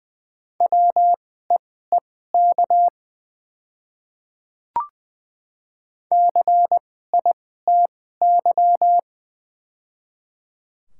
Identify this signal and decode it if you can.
Morse code, heard as a keyed tone, about 20 wpm, spaced as for 14 wpm, 700 Hz. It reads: WEEK CITY
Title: 2x Character Spacing - Top 500 Words - 20wpm